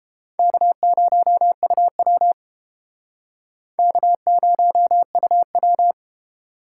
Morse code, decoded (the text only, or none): K0UW K0UW